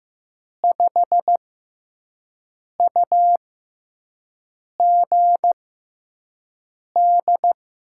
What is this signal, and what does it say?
Morse code, heard as a keyed tone, about 15 wpm, spaced as for 5 wpm, 700 Hz. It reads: 5UGD
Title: USA Call Signs - 6x Character Spacing - 15wpm